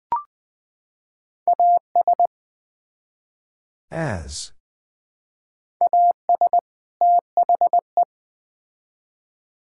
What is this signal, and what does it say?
Morse code, heard as a keyed tone, about 20 wpm, 700 Hz. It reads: AS AS THE
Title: Build A Phrase - 2-Word Phrases - 20wpm